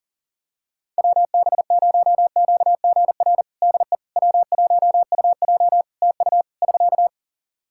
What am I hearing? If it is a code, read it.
Morse code, heard as a keyed tone, about 40 wpm, 700 Hz. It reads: WB0QGR DE W1UJ TU <SK>
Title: Practice QSO - Exchange 4 - Ending - 40wpm